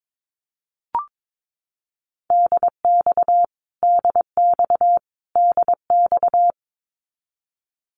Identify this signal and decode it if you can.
Morse code, heard as a keyed tone, about 22 wpm, 700 Hz. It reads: DX DX DX
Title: Abbreviations Repeated 3x - 22wpm